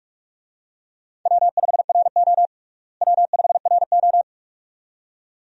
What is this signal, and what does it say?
Morse code, heard as a keyed tone, about 46 wpm, 700 Hz. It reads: W5RO W5RO
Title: USA Call Signs - 4 Characters - Repeated 2x - 45wpm